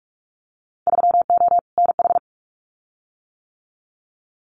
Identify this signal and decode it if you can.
Morse code, heard as a keyed tone, about 46 wpm, 690 Hz. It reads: 3O NH
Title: Winter Field Day Exchanges Encore 45wpm